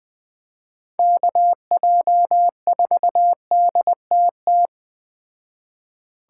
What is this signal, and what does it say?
Morse code, heard as a keyed tone, about 20 wpm, 700 Hz. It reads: KJ4DTT